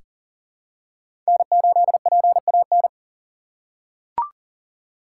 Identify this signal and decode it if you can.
Morse code, heard as a keyed tone, about 38 wpm, 700 Hz. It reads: N8PAN
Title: USA Call Signs - 5 Characters - 3x Character Spacing - 40wpm